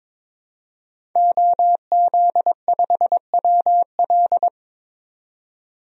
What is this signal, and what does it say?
Morse code, heard as a keyed tone, about 22 wpm, 700 Hz. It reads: OZ5WL
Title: DXCC-100 Call Signs - 5 Characters - 22wpm